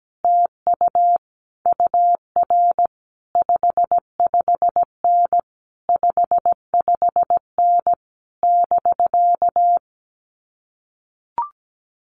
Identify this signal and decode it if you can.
Morse code, heard as a keyed tone, about 17 wpm, 700 Hz. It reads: TU UR 55N 55N <BK>
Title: Common QSO Phrases Encore 17wpm